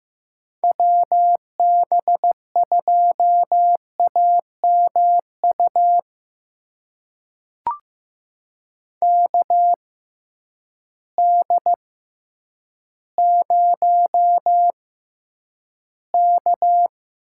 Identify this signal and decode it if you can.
Morse code, heard as a keyed tone, about 15 wpm, spaced as for 9 wpm, 700 Hz. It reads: WB2AMU KD0K
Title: USA Call Signs - 6 Characters - 6x Character Spacing - 15wpm